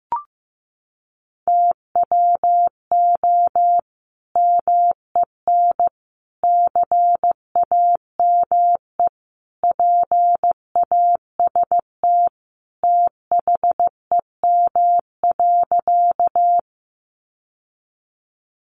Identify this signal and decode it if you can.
Morse code, heard as a keyed tone, about 15 wpm, 700 Hz. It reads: TWO MEN CAME PAST THEM.
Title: Sentences from Top 600 Words 15wpm